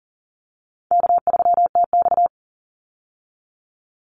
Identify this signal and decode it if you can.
Morse code, heard as a keyed tone, about 40 wpm, 700 Hz. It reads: K3TX
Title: Easy Call Signs Repeated 3x - 40wpm